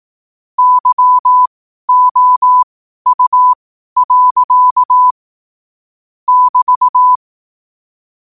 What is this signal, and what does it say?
Morse code, heard as a keyed tone, about 18 wpm, spaced as for 12 wpm, 1000 Hz. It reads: YOU. <BT>
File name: qotdfiles / 12wpm